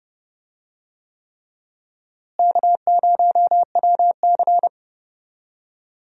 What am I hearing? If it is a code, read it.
Morse code, heard as a keyed tone, about 31 wpm, 700 Hz. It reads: K0WC
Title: USA Call Signs - 4 Characters - Repeated 3x - 30wpm